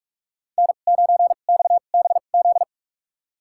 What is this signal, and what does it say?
Morse code, heard as a keyed tone, about 46 wpm, spaced as for 33 wpm, 700 Hz. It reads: N9XBZ
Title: USA Call Signs - 5 Characters - 2x Character Spacing - 45wpm